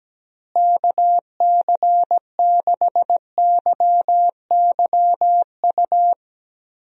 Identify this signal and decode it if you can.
Morse code, heard as a keyed tone, about 17 wpm, 700 Hz. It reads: KC6YYU
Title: USA Call Signs - 6 Characters - Repeated 2x - 17wpm